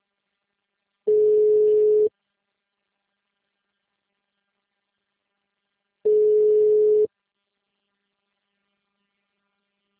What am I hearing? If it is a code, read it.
Ringback tone (Europe, 425 Hz).
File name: out- -2014